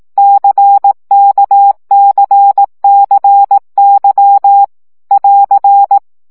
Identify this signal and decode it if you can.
Morse code, heard as a keyed tone, about 18 wpm, 800 Hz. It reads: CKCCY <AR>
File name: CW gyakorlas k c y 15 baud